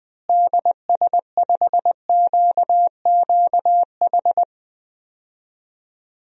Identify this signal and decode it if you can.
Morse code, heard as a keyed tone, about 20 wpm, 700 Hz. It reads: DS5QQH
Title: DXCC-100 Call Signs - 6 Characters - Repeated 3x - 20wpm